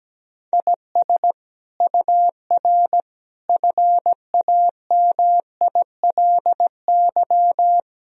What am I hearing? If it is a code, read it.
Morse code, heard as a keyed tone, about 17 wpm, 700 Hz. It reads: IS UR FAMILY
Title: Sentences from Top 400 Words and Abbreviations 17wpm